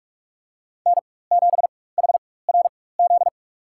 Morse code, heard as a keyed tone, about 46 wpm, spaced as for 21 wpm, 700 Hz. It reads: N7HRZ
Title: USA Call Signs - 4x Character Spacing - 45wpm